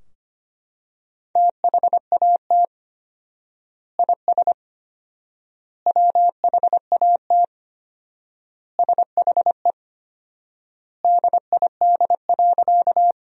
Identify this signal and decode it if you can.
Morse code, heard as a keyed tone, about 25 wpm, 700 Hz. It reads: THAT IS WHAT SHE DID.
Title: 4x Word-Spacing - Sentences from Top 100 Words - 25wpm